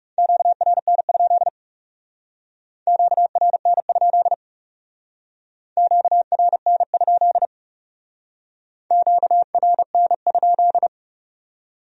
Morse code, heard as a keyed tone, about 38 wpm, 700 Hz. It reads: QRN? QRN? QRN? QRN?
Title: Speed-Racing Q-Codes - 30wpm